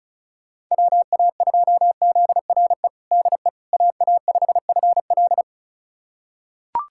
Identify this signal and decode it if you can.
Morse code, heard as a keyed tone, about 36 wpm, 700 Hz. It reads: WA2ZRE DE AA5FL
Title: USA Callsign DE Callsign 35wpm